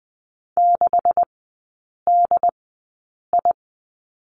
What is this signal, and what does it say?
Morse code, heard as a keyed tone, about 20 wpm, spaced as for 8 wpm, 700 Hz. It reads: 6DI
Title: 127 Lesson 21e - Number 6 - Three Characters 20wpm